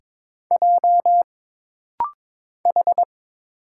Morse code, heard as a keyed tone, about 22 wpm, 700 Hz. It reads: J H
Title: Single Letter-Number Alt - Rapid-Fire 22wpm